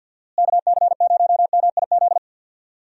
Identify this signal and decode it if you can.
Morse code, heard as a keyed tone, about 50 wpm, 700 Hz. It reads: KC0MIZ
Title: USA Call Signs - 6 Characters - Repeated 2x - 50wpm